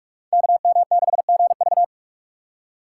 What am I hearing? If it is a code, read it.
Morse code, heard as a keyed tone, about 44 wpm, 700 Hz. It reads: KM6GV
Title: USA Call Signs - 5 Characters - 4x Character Spacing - 45wpm